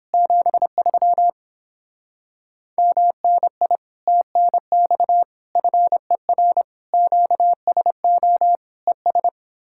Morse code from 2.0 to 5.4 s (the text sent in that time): MNI TNX